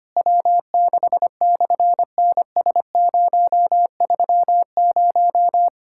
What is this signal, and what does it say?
Morse code, heard as a keyed tone, about 26 wpm, 700 Hz. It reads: W6/NS030
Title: SOTA QSO Practice 25wpm